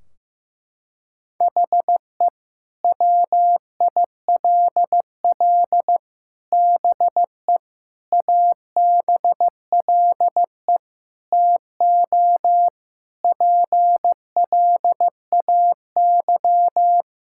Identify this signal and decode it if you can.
Morse code, heard as a keyed tone, about 15 wpm, 700 Hz. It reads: HE WILL BE ABLE TO PLAY
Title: Sentences from Top 1600 Words 15wpm